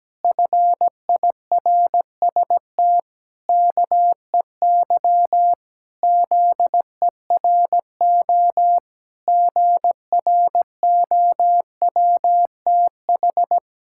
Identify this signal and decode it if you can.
Morse code, heard as a keyed tone, about 17 wpm, 700 Hz. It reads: FIRST KEY ZERO GROWTH
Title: Pangram Challenge v2 - 17wpm